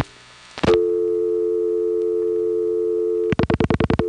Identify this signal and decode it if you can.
Dial tone (North America), 350 + 440 Hz.